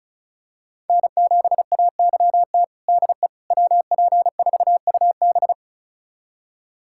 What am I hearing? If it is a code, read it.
Morse code, heard as a keyed tone, about 34 wpm, 700 Hz. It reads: N7AYT DE WP4UB
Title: USA Callsign DE Callsign - 5 Characters - 35wpm